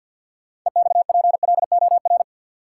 Morse code, heard as a keyed tone, about 49 wpm, 700 Hz. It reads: EXPLOR
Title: Sentences from Top 1100 Words - New Word Drill - 50wpm